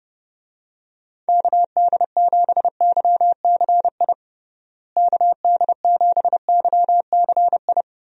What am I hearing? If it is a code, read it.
Morse code, heard as a keyed tone, about 32 wpm, 700 Hz. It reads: KD7YCI KD7YCI
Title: USA Call Signs - 6 Characters - Repeated 2x - 30wpm